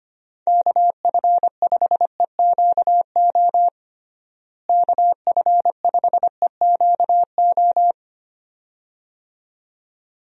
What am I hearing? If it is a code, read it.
Morse code, heard as a keyed tone, about 25 wpm, 700 Hz. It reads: KF5EQO KF5EQO
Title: USA Call Signs - 6 Characters - Repeated 2x - 25wpm